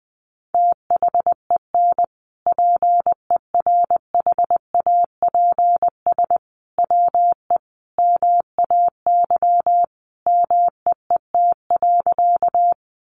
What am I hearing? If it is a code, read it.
Morse code, heard as a keyed tone, about 20 wpm, 700 Hz. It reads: THEN PERHAPS WE MAY MEET.
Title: Sentences from Top 1100 Words 20wpm